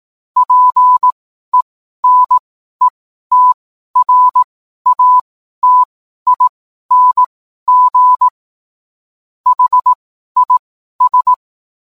Morse code, heard as a keyed tone, about 18 wpm, spaced as for 12 wpm, 1000 Hz. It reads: PENETRATING HIS